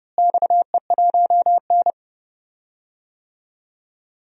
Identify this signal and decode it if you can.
Morse code, heard as a keyed tone, about 30 wpm, 700 Hz. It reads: XE1N
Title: DXCC-100 Call Signs - 4 Characters - Repeated 2x - 30wpm